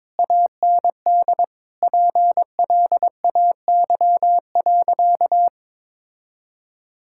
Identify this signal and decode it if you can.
Morse code, heard as a keyed tone, about 22 wpm, 700 Hz. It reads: AND PLAY.
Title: Sentences from Top 1000 Words - Review Part 6 - 22wpm